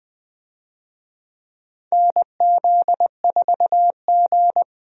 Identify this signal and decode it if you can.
Morse code, heard as a keyed tone, about 20 wpm, 700 Hz. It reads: NZ4G
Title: USA Call Signs - 4 Characters - 20wpm